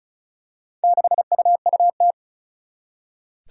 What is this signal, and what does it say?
Morse code, heard as a keyed tone, about 36 wpm, 700 Hz. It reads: 6UUT
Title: Four Challenging Letter-Number Combinations 35wpm